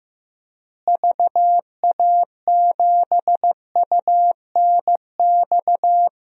Morse code, heard as a keyed tone, about 15 wpm, 700 Hz. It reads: VA7UNX
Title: DXCC-100 Call Signs - 6 Characters - Repeated 2x - 15wpm